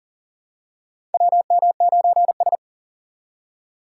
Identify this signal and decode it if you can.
Morse code, heard as a keyed tone, about 37 wpm, 700 Hz. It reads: WM9S
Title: USA Call Signs - 4 Characters - 4x Character Spacing - 40wpm